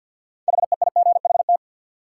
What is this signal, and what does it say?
Morse code, heard as a keyed tone, about 53 wpm, 700 Hz. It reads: HEIGHT
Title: Sentences from Top 1600 Words - New Word Drill Alt - 50wpm